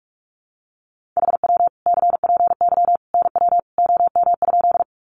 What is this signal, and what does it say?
Morse code, heard as a keyed tone, about 44 wpm, 700 Hz. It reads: HW CPY NW OM?
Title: Common QSO Phrases 45wpm